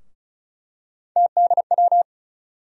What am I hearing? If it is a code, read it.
Morse code, heard as a keyed tone, about 36 wpm, 700 Hz. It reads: TDW
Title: USA Call Sign Postfixes - 3 Letters 35wpm